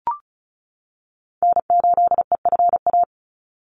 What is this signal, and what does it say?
Morse code, heard as a keyed tone, about 34 wpm, 700 Hz. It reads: N8EFA
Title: USA Call Signs 35wpm